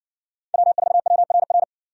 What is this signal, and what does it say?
Morse code, heard as a keyed tone, about 56 wpm, 700 Hz. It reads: W4WRR